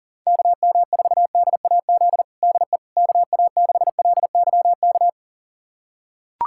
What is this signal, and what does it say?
Morse code, heard as a keyed tone, about 40 wpm, 700 Hz. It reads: KM4DAZ DE KA6LYK
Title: USA Callsign DE Callsign 40wpm